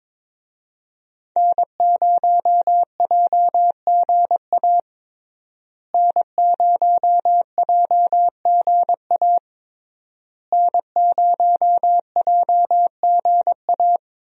Morse code, heard as a keyed tone, about 22 wpm, 700 Hz. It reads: N0JGA N0JGA N0JGA
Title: USA Call Signs - 5 Characters - Repeated 3x - 22wpm